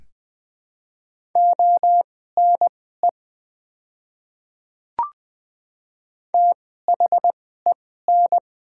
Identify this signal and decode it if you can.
Morse code, heard as a keyed tone, about 20 wpm, spaced as for 14 wpm, 700 Hz. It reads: ONE THEN
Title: 2x Character Spacing - Top 100 Words - 20wpm